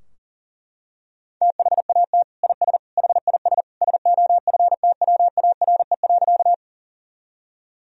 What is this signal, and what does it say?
Morse code, heard as a keyed tone, about 40 wpm, 700 Hz. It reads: THAT IS HIS SOFTWARE.
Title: Sentences from Top 1000 Words - Review Part 3 - 40wpm